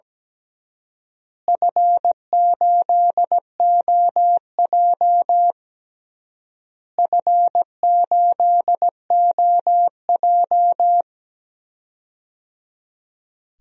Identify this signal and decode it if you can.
Morse code, heard as a keyed tone, about 17 wpm, 700 Hz. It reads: F8OJ F8OJ
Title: DXCC-100 Call Signs - 4 Characters - Repeated 3x - 17wpm